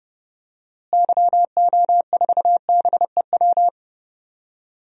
Morse code, heard as a keyed tone, about 30 wpm, 700 Hz. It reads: YO4BEW
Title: DXCC-100 Call Signs - 6 Characters - Repeated 3x - 30wpm